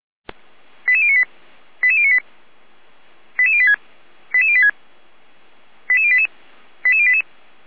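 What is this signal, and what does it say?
Five-tone selective call (ZVEI), heard as quick runs of five tones, tones of about 70 ms. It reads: 88098 88098 88086 88086 88088 88088
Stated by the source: ZVEI five-tone call 88571 received by SDR.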